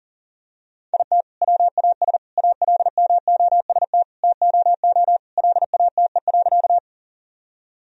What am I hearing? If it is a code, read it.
Morse code, heard as a keyed tone, about 40 wpm, 700 Hz. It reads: IT WAS ALMOST TOO LATE.
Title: Sentences from Top 1000 Words - Review Part 4 - 40wpm